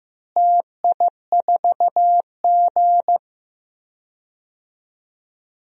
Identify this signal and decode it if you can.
Morse code, heard as a keyed tone, about 15 wpm, 700 Hz. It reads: TI4G
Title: DXCC-100 Call Signs - 4 Characters - Repeated 3x - 15wpm